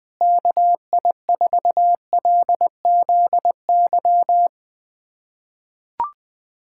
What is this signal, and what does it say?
Morse code, heard as a keyed tone, about 20 wpm, 700 Hz. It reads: KI4LZY
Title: USA Callsign DE Callsign - 6 Characters - 20wpm